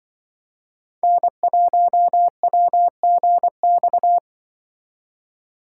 Morse code, heard as a keyed tone, about 24 wpm, 700 Hz. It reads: N1WGX
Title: Speed Racing - USA 1x3 Call Signs 20wpm